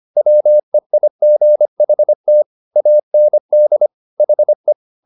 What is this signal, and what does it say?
Morse code, heard as a keyed tone, about 25 wpm, 600 Hz. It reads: WEIGHT AND HE
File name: the old man and the sea part 1 25wpm